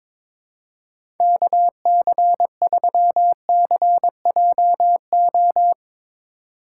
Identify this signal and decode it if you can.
Morse code, heard as a keyed tone, about 21 wpm, 700 Hz. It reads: KC3CJO
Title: USA Call Signs - 6 Characters - 2x Character Spacing - 22wpm